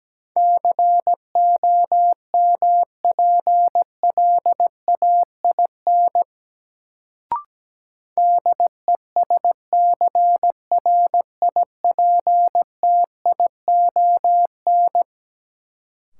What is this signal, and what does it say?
Morse code, heard as a keyed tone, about 17 wpm, 700 Hz. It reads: COMPLAIN DESCRIPTION
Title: Sentences from Top 1300 Words - New Word Drill Alt - 17wpm